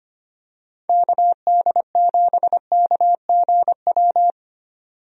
Morse code, heard as a keyed tone, about 25 wpm, 700 Hz. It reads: KD7KGW
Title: USA Call Signs - 6 Characters - 25wpm